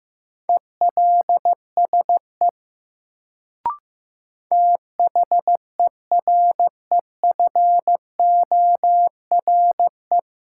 Morse code, heard as a keyed tone, about 15 wpm, 700 Hz. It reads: ELSE THEREFORE
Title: Sentences from Top 800 Words - New Word Drill - 15wpm